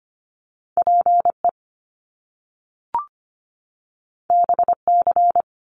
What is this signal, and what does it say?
Morse code, heard as a keyed tone, about 25 wpm, 700 Hz. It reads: PE BC
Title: Canadian Provinces and Territories 25wpm